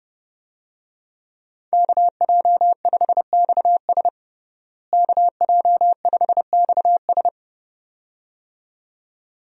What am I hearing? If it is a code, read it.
Morse code, heard as a keyed tone, about 32 wpm, 700 Hz. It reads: KJ5XS KJ5XS
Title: USA Call Signs - 5 Characters - Repeated 2x - 30wpm